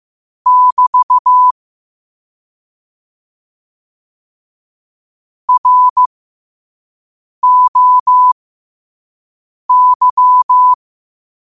Morse code, heard as a keyed tone, about 15 wpm, spaced as for 5 wpm, 1000 Hz. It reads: <BT> ROY
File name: qotdfiles / 5wpm